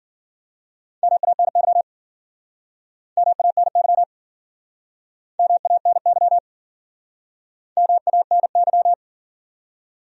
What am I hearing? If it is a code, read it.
Morse code, heard as a keyed tone, about 54 wpm, 700 Hz. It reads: MANY MANY MANY MANY